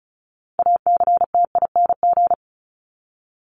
Morse code, heard as a keyed tone, about 34 wpm, 700 Hz. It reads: ACTING
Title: ING Suffix - Top 100 Words with Base Words 35wpm